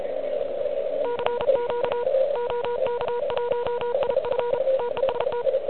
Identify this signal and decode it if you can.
Morse code, heard as a keyed tone, about 34 wpm, 960 Hz, with noise. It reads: CQ OK1IF TEST
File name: ARCHIV / QCX / QCX Pileup